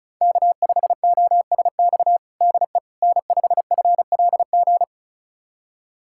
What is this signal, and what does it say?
Morse code, heard as a keyed tone, about 34 wpm, 700 Hz. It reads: K5OSX DE N5FLG
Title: USA Callsign DE Callsign - 5 Characters - 35wpm